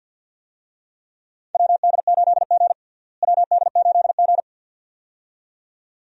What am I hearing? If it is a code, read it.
Morse code, heard as a keyed tone, about 53 wpm, 700 Hz. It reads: WD8G WD8G
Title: USA Call Signs - 4 Characters - Repeated 2x - 50wpm